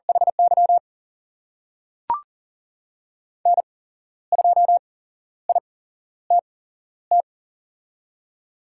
Morse code, heard as a keyed tone, about 38 wpm, spaced as for 13 wpm, 700 Hz. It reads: HY N2ITT
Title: USA Call Signs - 8x Character Spacing - 40wpm